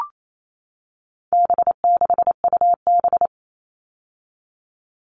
Morse code, heard as a keyed tone, about 28 wpm, 700 Hz. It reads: B6UB
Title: All Cut Numbers - 4-Digit 28wpm